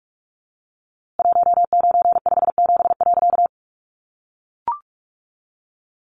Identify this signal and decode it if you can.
Morse code, heard as a keyed tone, about 45 wpm, 700 Hz. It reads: 1957.
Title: A Sound Of Thunder 45wpm